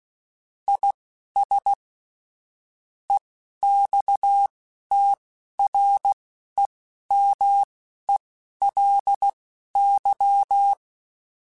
Morse code, heard as a keyed tone, about 16 wpm, spaced as for 11 wpm, 800 Hz. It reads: IS EXTREMELY
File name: Technical Subjects / Technical Subjects-mp3s / AND Gate @10